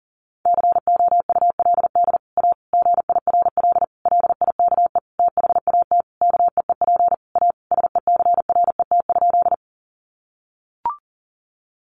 Morse code, heard as a keyed tone, about 40 wpm, 700 Hz. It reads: COULD A GIRL LIKE THAT KEEP A SECRET?
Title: Sentences from Top 1200 Words 40wpm